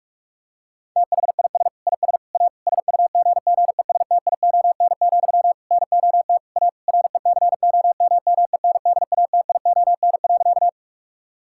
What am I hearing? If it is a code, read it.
Morse code, heard as a keyed tone, about 46 wpm, 700 Hz. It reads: THIS IS A SUGGESTION, NOT A RECOMMENDATION.
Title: Sentences from Top 1600 Words 45wpm